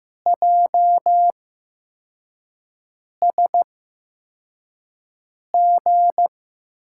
Morse code, heard as a keyed tone, about 15 wpm, spaced as for 4 wpm, 700 Hz. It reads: JSG
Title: USA Call Signs - 8x Character Spacing - 15wpm